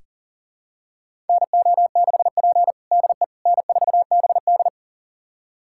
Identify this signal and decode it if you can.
Morse code, heard as a keyed tone, about 37 wpm, 690 Hz. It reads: NO6P DE N4BD
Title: USA Callsign DE Callsign - 4 Characters - 40wpm